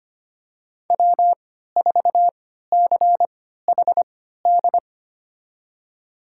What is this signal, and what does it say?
Morse code, heard as a keyed tone, about 25 wpm, spaced as for 14 wpm, 700 Hz. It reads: W4CHD
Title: USA Call Signs - 5 Characters - 3x Character Spacing - 25wpm